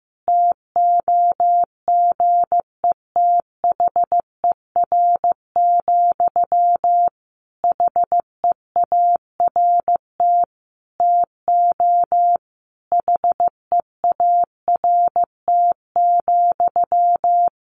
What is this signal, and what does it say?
Morse code, heard as a keyed tone, about 14 wpm, 700 Hz. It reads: TOGETHER, HEART TO HEART,
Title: Sentences from Top 1500 Words 15wpm